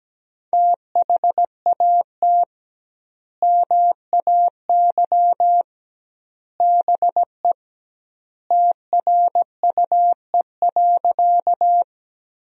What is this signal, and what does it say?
Morse code, heard as a keyed tone, about 17 wpm, 700 Hz. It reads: THAT MAY BE TRUE.
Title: 2x Word-Spacing - Sentences from Top 400 Words - 17wpm